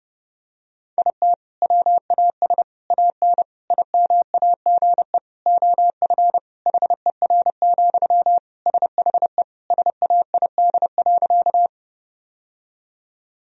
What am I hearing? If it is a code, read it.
Morse code, heard as a keyed tone, about 32 wpm, 700 Hz. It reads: IT WAS AN IMAGE OF HER, SHE SAID.
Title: Sentences from Top 1000 Words - Review Part 4 - 30wpm